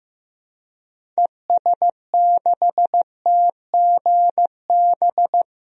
Morse code, heard as a keyed tone, about 15 wpm, 700 Hz. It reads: ES6TGB
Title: DXCC-100 Call Signs - 6 Characters - 15wpm